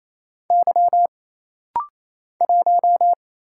Morse code, heard as a keyed tone, about 27 wpm, 700 Hz. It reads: Y 1
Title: Single Letter-Number - Rapid-Fire 28wpm